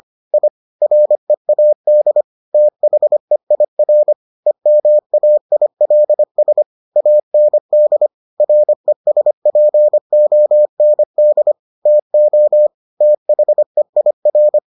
Morse code, heard as a keyed tone, about 25 wpm, 600 Hz. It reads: I READ THEIR EMAILS AND RESPOND TO THEIR